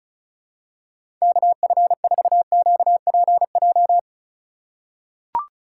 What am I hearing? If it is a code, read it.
Morse code, heard as a keyed tone, about 36 wpm, 700 Hz. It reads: KF4QPJ
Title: USA Call Signs - 6 Characters - 2x Character Spacing - 35wpm